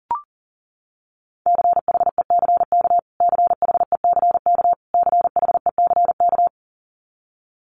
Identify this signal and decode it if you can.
Morse code, heard as a keyed tone, about 40 wpm, 700 Hz. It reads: CHECK CHECK CHECK
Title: Top 500 Words Repeated 40wpm